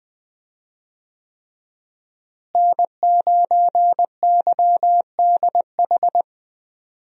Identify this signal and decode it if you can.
Morse code, heard as a keyed tone, about 20 wpm, 700 Hz. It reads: N9YDH
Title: USA Call Signs - 5 Characters - Repeated 3x - 20wpm